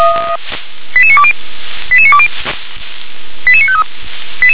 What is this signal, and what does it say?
Five-tone selective call (ZVEI), heard as quick runs of five tones, tones of about 70 ms. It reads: 88920 88920 88953